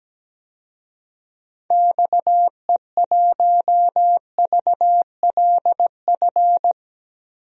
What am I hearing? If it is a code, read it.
Morse code, heard as a keyed tone, about 17 wpm, 700 Hz. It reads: XE1VLF
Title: DXCC-100 Call Signs - 6 Characters - 17wpm